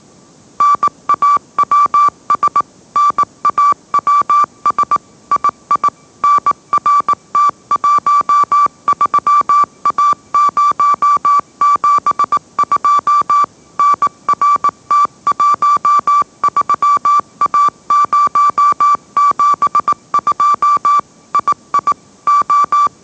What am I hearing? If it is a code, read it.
Morse code, heard as a keyed tone, about 18 wpm, 1160 Hz. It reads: NAWS NAWS II NRT13A072 NRT13A072 II O
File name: mrx / mrx082 20wpm